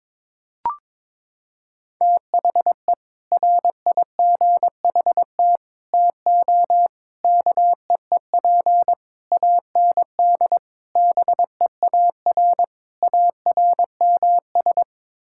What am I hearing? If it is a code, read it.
Morse code, heard as a keyed tone, about 22 wpm, 700 Hz. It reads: THE RIGHT TO KEEP AND BEAR ARMS
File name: practice / 22wpm / Sets of 7 Words - 22wpm